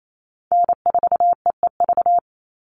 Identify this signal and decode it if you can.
Morse code, heard as a keyed tone, about 28 wpm, 700 Hz. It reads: N4EEV
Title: Call Signs - 28wpm